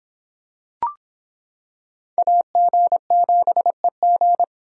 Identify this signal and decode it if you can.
Morse code, heard as a keyed tone, about 26 wpm, 700 Hz. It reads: AG7EG